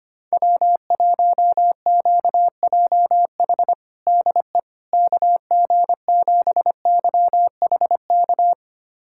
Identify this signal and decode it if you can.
Morse code, heard as a keyed tone, about 25 wpm, 700 Hz. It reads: W1QJH DE KG7YHK
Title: USA Callsign DE Callsign 25wpm